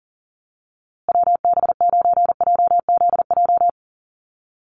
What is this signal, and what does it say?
Morse code, heard as a keyed tone, about 40 wpm, 700 Hz. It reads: WB9JZJ
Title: USA Call Signs Repeated 3x 40wpm